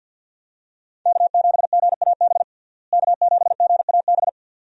Morse code, heard as a keyed tone, about 52 wpm, 700 Hz. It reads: K7GAB K7GAB
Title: USA Call Signs - 5 Characters - Repeated 2x - 50wpm